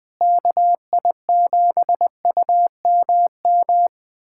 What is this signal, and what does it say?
Morse code, heard as a keyed tone, about 20 wpm, 700 Hz. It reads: KI7UMM
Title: USA Callsign DE Callsign 20wpm